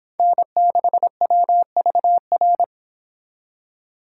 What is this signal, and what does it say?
Morse code, heard as a keyed tone, about 25 wpm, 700 Hz. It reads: N6WVR